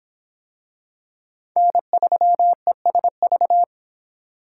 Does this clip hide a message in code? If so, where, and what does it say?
Yes: Morse code from 1.6 to 3.6 s, heard as a keyed tone, about 26 wpm, 700 Hz. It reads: N3ESV